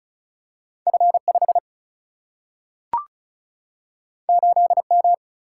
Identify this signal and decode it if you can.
Morse code, heard as a keyed tone, about 36 wpm, 700 Hz. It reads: F5 8M
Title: Single-Letter Single-Number 35wpm